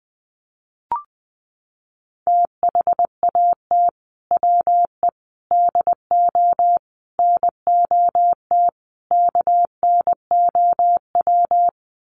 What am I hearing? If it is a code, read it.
Morse code, heard as a keyed tone, about 20 wpm, 700 Hz. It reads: THAT WE DO NOT KNOW
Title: Sets of 5 Words Encore2 20wpm